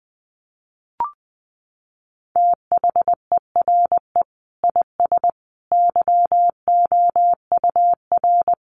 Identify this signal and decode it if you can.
Morse code, heard as a keyed tone, about 20 wpm, 700 Hz. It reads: THERE IS YOUR
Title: Sentences from Top 500 Words 20wpm